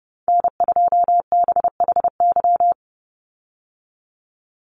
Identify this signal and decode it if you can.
Morse code, heard as a keyed tone, about 30 wpm, 700 Hz. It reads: N2BHY
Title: USA Call Signs 30wpm